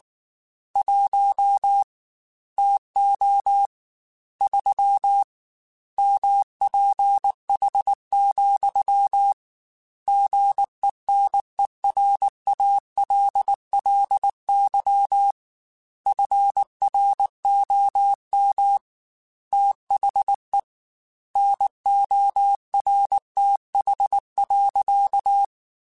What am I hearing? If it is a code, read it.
Morse code, heard as a keyed tone, about 19 wpm, 800 Hz. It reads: 1 TO 3 MPH, GENERALLY FROM THE NORTH.